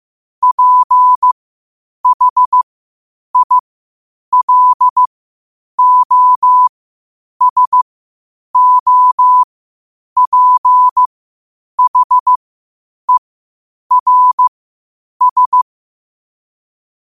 Morse code, heard as a keyed tone, about 15 wpm, spaced as for 8 wpm, 1000 Hz. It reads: PHILOSOPHERS
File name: qotdfiles / 8wpm